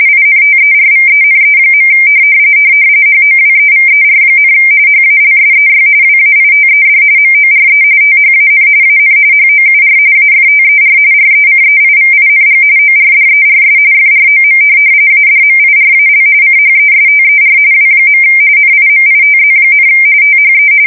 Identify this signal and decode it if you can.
RTTY, about 45.45 baud. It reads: INTERVENE. HOWEVER, HALSEY WAITED MORE THAN AN HOUR AFTER RECEIVING ORDERS FROM ADMIRAL CHESTER W. NIMITZ, THE
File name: fsk170 / USS-Massachusetts